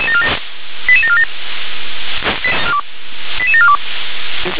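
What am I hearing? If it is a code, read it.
Five-tone selective call (ZVEI), heard as quick runs of five tones, tones of about 70 ms. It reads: 88747 88742
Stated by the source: ZVEI five-tone call 88702 received by SDR.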